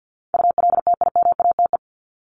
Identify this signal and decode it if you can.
Morse code, heard as a keyed tone, about 50 wpm, 700 Hz. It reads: ULTIMATE
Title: LY Suffix - Top 100 Words with Base Words 50wpm